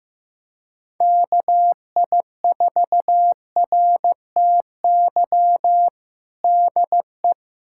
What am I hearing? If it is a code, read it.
Morse code, heard as a keyed tone, about 15 wpm, 700 Hz. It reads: KI4RTY DE
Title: USA Callsign DE Callsign - 6 Characters - 15wpm